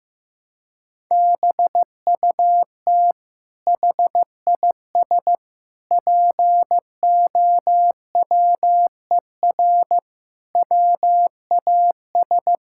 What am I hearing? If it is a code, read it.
Morse code, heard as a keyed tone, about 16 wpm, 700 Hz. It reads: BUT HIS POWER WAS
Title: Sentences from Top 1000 Words - Review Part 6 - 15wpm